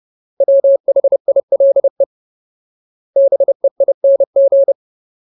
Morse code, heard as a keyed tone, about 30 wpm, 550 Hz. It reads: WHILE BEING
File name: Ultimate Word List / 30WPM Ultimate